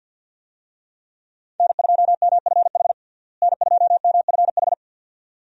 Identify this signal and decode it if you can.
Morse code, heard as a keyed tone, about 51 wpm, 700 Hz. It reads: N2MFH N2MFH
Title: USA Call Signs - 5 Characters - Repeated 2x - 50wpm